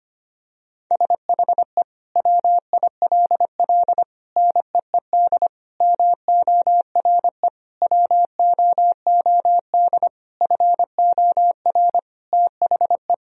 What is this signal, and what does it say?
Morse code, heard as a keyed tone, about 25 wpm, 700 Hz. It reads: SHE WILL NEED MORE WOOD FOR THE
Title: Sentences from Top 1000 Words - Review Part 2 - 25wpm